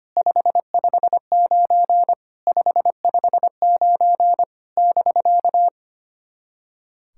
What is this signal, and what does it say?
Morse code, heard as a keyed tone, about 25 wpm, 700 Hz. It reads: 559 559 <BK>
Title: SOTA QSO Practice 25wpm